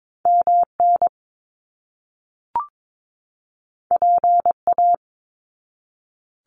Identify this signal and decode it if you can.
Morse code, heard as a keyed tone, about 22 wpm, 700 Hz. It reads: MN PA
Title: US States 22wpm